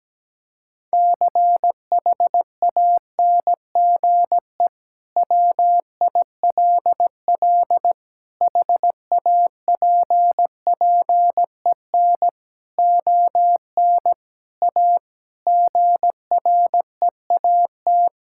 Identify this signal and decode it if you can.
Morse code, heard as a keyed tone, about 17 wpm, 700 Hz. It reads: CHANGE WILL HAPPEN ON A GREAT
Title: Sentences from Top 1000 Words - Review Part 5 - 17wpm